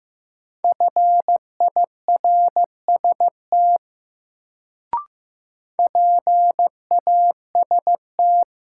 Morse code, heard as a keyed tone, about 15 wpm, 700 Hz. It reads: FIRST PAST
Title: Sets of 2 Words - New Word Drill - 15wpm